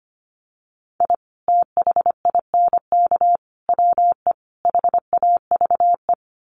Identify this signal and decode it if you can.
Morse code, heard as a keyed tone, about 25 wpm, 700 Hz. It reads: I THINK WE HAVE
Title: Sets of 4 Words 25wpm